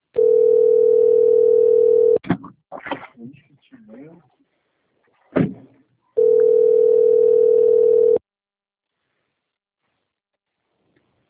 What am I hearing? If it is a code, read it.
Ringback tone (North America), 440 + 480 Hz, with speech.
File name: rg-1600-unknown- - - .36037